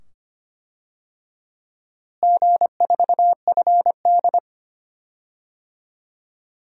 Morse code, heard as a keyed tone, about 25 wpm, 700 Hz. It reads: G4FD
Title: DXCC-100 Call Signs - 4 Characters - Repeated 2x - 25wpm